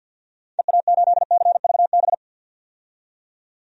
Morse code, heard as a keyed tone, about 46 wpm, 700 Hz. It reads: EA8CVB
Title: DXCC-100 Call Signs - 6 Characters - 50wpm